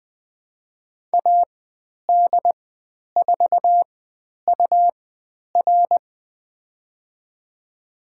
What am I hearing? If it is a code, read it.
Morse code, heard as a keyed tone, about 20 wpm, spaced as for 10 wpm, 700 Hz. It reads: AD4UR
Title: 109 Lesson 18g - Letter m - Call Signs 20wpm